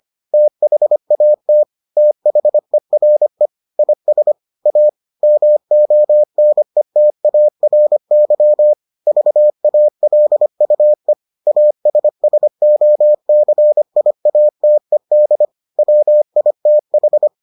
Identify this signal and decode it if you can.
Morse code, heard as a keyed tone, about 25 wpm, 600 Hz. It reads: THAT THERE IS A MONETARY VALUE ASSOCIATED WITH